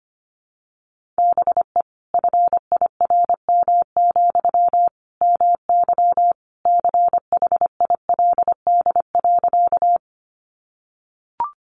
Morse code, heard as a keyed tone, about 25 wpm, 700 Hz. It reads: BE FIRM, MY CHILD.
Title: Sentences from Top 1400 Words 25wpm